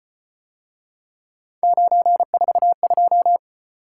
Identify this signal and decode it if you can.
Morse code, heard as a keyed tone, about 35 wpm, 700 Hz. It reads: 942